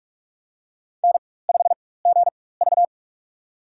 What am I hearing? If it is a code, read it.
Morse code, heard as a keyed tone, about 44 wpm, spaced as for 21 wpm, 690 Hz. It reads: N5GV
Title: USA Call Signs - 4x Character Spacing - 45wpm